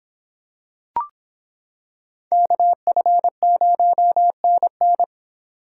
Morse code, heard as a keyed tone, about 26 wpm, 700 Hz. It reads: KF0NN